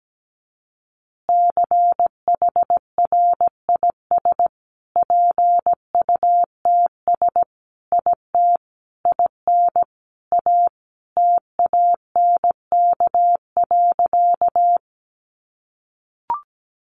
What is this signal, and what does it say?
Morse code, heard as a keyed tone, about 17 wpm, 700 Hz. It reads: CHRIS PUTS IT IN A TANK.
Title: Easy Phrases 17wpm